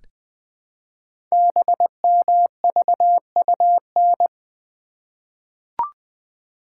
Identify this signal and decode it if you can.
Morse code, heard as a keyed tone, about 20 wpm, 700 Hz. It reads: BMVUN
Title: Code Groups - 5 Letters - 20wpm